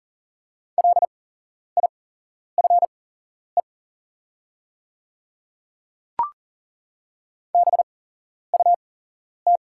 Morse code, heard as a keyed tone, about 40 wpm, spaced as for 11 wpm, 700 Hz. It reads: LIFE BUT
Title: 8x Character Spacing - Top 300 Words - 40wpm